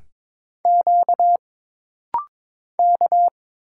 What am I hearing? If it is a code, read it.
Morse code, heard as a keyed tone, about 22 wpm, 700 Hz. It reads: Q K
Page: This rapid-fire timing is geared to push your ICR (Instant Character Recognition) skills.